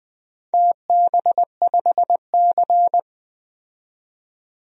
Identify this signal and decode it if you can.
Morse code, heard as a keyed tone, about 20 wpm, 700 Hz. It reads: TB5C
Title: DXCC-100 Call Signs - 4 Characters - 20wpm